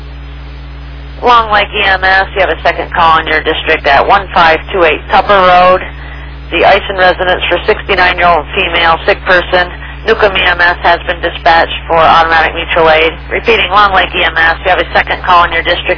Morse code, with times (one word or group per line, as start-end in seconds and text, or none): none